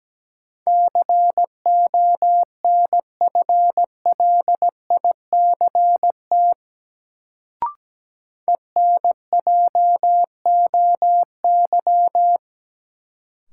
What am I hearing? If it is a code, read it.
Morse code, heard as a keyed tone, about 17 wpm, 700 Hz. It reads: CONFLICT ENJOY
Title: Sentences from Top 1300 Words - New Word Drill Alt - 17wpm